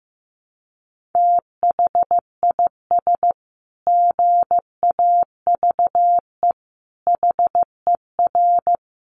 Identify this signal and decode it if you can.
Morse code, heard as a keyed tone, about 15 wpm, 700 Hz. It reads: THIS GAVE HER
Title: Sentences from Top 1400 Words 15wpm